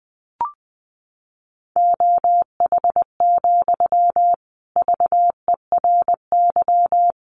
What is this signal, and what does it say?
Morse code, heard as a keyed tone, about 20 wpm, 700 Hz. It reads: OH, VERY
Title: Sentences from Top 500 Words 20wpm